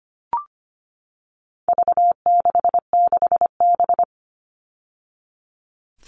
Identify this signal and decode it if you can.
Morse code, heard as a keyed tone, about 25 wpm, 700 Hz. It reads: V66B
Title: All Cut Numbers - 4-Digit 25wpm